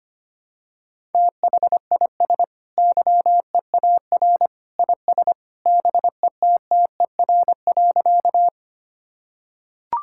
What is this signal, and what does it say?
Morse code, heard as a keyed tone, about 25 wpm, 700 Hz. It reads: THIS YEAR IS BETTER.
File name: practice / 25wpm / Sentences with 2-Word Phrases - 25wpm